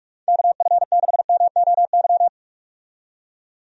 Morse code, heard as a keyed tone, about 46 wpm, 700 Hz. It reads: KF6MOY
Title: USA Call Signs - 6 Characters - 45wpm